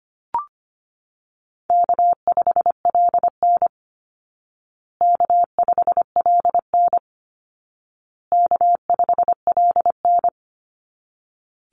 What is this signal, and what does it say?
Morse code, heard as a keyed tone, about 25 wpm, 700 Hz. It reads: K5LN K5LN K5LN
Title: Easy Call Signs Repeated 3x - 25wpm